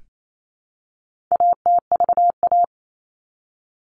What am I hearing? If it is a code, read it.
Morse code, heard as a keyed tone, about 28 wpm, 700 Hz. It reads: ATVA
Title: All Cut Numbers - 4-Digit 28wpm